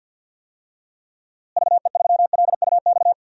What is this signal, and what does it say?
Morse code, heard as a keyed tone, about 49 wpm, 700 Hz. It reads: VE3LUX
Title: DXCC-100 Call Signs - 6 Characters - Repeated 3x - 50wpm